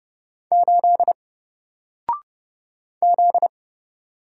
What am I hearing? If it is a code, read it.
Morse code, heard as a keyed tone, about 30 wpm, 700 Hz. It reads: 8 Z
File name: Single-Letter-Number-30wpm